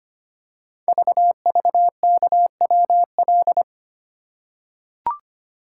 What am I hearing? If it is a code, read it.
Morse code, heard as a keyed tone, about 25 wpm, 700 Hz. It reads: VVKWL
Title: Code Groups - 5 Letters - 25wpm